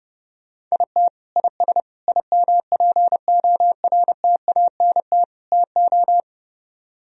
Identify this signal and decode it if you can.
Morse code, heard as a keyed tone, about 34 wpm, 700 Hz. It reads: IT IS IMPORTANT TO
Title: Speed-Racing - Sets of 4 Words 30wpm